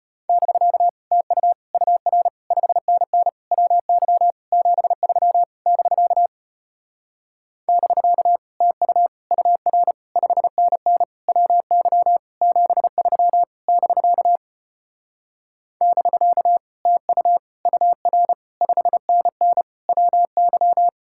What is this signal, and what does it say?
Morse code, heard as a keyed tone, about 33 wpm, 700 Hz. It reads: <BK> TU UR 5NN WY 73 <BK> <BK> TU UR 5NN WY 73 <BK> <BK> TU UR 5NN WY